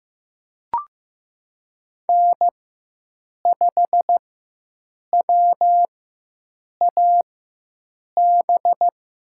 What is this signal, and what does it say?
Morse code, heard as a keyed tone, about 15 wpm, spaced as for 7 wpm, 700 Hz. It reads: N5WAB
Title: USA Call Signs - 5 Characters - 4x Character Spacing - 15wpm